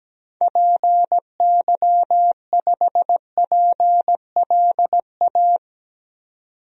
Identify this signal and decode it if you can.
Morse code, heard as a keyed tone, about 17 wpm, 700 Hz. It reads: PY5PLA
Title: DXCC-100 Call Signs - 6 Characters - 17wpm